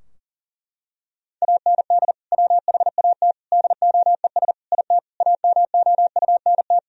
Morse code, heard as a keyed tone, about 40 wpm, 700 Hz. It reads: AND WHAT DOES IT AMOUNT
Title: Sentences from Top 1000 Words - Review Part 4 - 40wpm